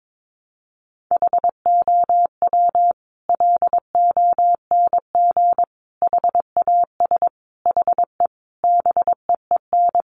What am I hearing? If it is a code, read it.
Morse code, heard as a keyed tone, about 22 wpm, 700 Hz. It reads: HOW LONG HAS HE BEEN
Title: Sentences from Top 100 Words 22wpm